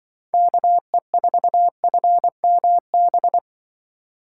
Morse code, heard as a keyed tone, about 24 wpm, 700 Hz. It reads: KE4FMB
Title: Speed Racing - USA 2x3 Call Signs 20wpm